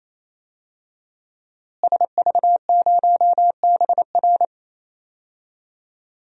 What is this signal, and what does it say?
Morse code, heard as a keyed tone, about 28 wpm, 690 Hz. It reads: SV0BR
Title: DXCC-100 Call Signs - 5 Characters - Repeated 3x - 28wpm